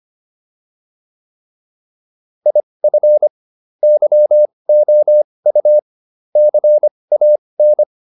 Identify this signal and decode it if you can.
Morse code, heard as a keyed tone, about 25 wpm, spaced as for 20 wpm, 600 Hz. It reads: IF YOU CAN
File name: WD304 20